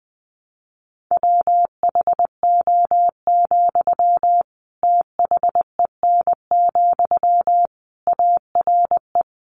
Morse code, heard as a keyed tone, about 19 wpm, 700 Hz. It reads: WHO, THEN, ARE
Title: Sentences from Top 100 Words 20wpm